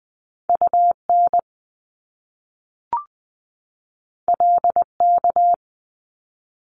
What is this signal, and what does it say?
Morse code, heard as a keyed tone, about 20 wpm, 700 Hz. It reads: UN LK
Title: Top Two Letter Combinations in Words 20wpm